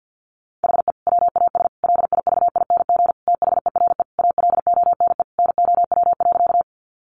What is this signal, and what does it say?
Morse code, heard as a keyed tone, about 50 wpm, 700 Hz. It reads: HE WAS LIVING THERE ALONE NOW.
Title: Sentences from Top 1100 Words 50wpm